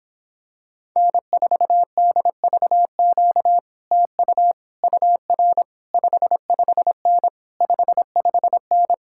Morse code, heard as a keyed tone, about 26 wpm, 700 Hz. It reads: N4DVQ TU UR 55N 55N